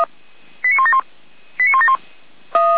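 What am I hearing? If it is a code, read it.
Five-tone selective call (ZVEI), heard as quick runs of five tones, tones of about 70 ms. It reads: 89171 89171
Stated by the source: SDR-received ZVEI five-tone selective call, code 89560.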